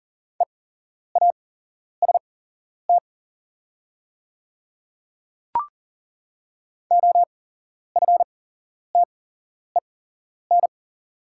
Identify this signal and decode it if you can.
Morse code, heard as a keyed tone, about 39 wpm, spaced as for 11 wpm, 700 Hz. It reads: EAST OFTEN
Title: 8x Character Spacing - Top 500 Words - 40wpm